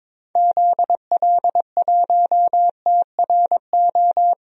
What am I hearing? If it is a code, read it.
Morse code, heard as a keyed tone, about 22 wpm, 700 Hz. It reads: ZL1TRO
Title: DXCC-100 Call Signs - 6 Characters - Repeated 2x - 22wpm